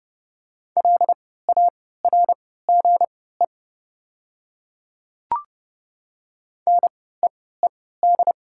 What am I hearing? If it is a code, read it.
Morse code, heard as a keyed tone, about 30 wpm, spaced as for 17 wpm, 700 Hz. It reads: LARGE NEED
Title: 3x Character Spacing - Top 200 Words - 30wpm